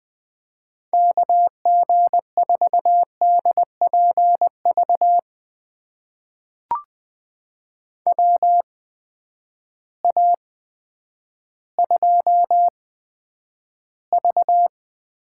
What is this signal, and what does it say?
Morse code, heard as a keyed tone, about 20 wpm, spaced as for 10 wpm, 700 Hz. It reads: KG4DPV WA2V
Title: USA Call Signs - 6 Characters - 8x Character Spacing - 20wpm